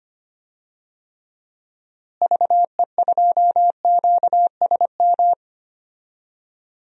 Morse code, heard as a keyed tone, about 25 wpm, 700 Hz. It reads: VE2QSM
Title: DXCC-100 Call Signs - 6 Characters - Repeated 2x - 25wpm